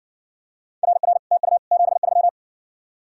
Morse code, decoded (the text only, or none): RR TU 73